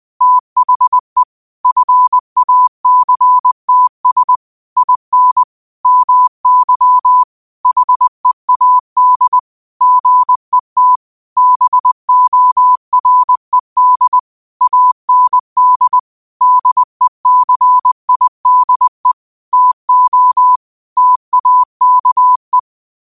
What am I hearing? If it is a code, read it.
Morse code, heard as a keyed tone, about 20 wpm, 1000 Hz. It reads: THE FACTS IN MY HEAD GET BORED AND DECIDE TO TAKE